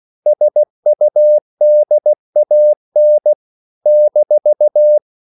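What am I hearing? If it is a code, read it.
Morse code, heard as a keyed tone, about 16 wpm, 600 Hz. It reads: SUDAN -
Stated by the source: CW practice of national capitals at 16 words per minute, from ON6ZQ.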